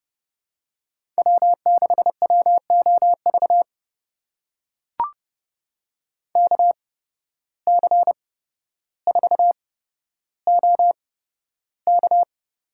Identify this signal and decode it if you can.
Morse code, heard as a keyed tone, about 28 wpm, spaced as for 13 wpm, 700 Hz. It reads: W6WOV KC4OK
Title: USA Call Signs - 5 Characters - 8x Character Spacing - 30wpm